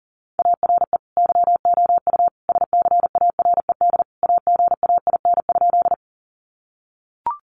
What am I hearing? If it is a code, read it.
Morse code, heard as a keyed tone, about 40 wpm, 700 Hz. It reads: ARE YOU SCARED AGAIN?
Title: Sentences from Top 700 Words 40wpm